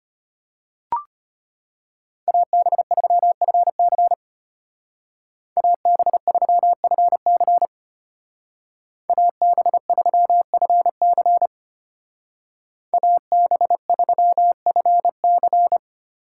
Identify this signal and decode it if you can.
Morse code, heard as a keyed tone, about 33 wpm, 700 Hz. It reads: AB3FC AB3FC AB3FC AB3FC